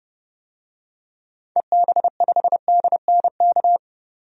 Morse code, heard as a keyed tone, about 30 wpm, 700 Hz. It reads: EB5DNK
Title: DXCC-100 Call Signs - 6 Characters - 30wpm